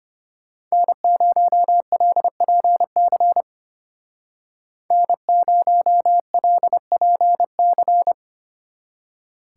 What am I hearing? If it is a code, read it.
Morse code, heard as a keyed tone, about 33 wpm, 700 Hz. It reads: N0LPC N0LPC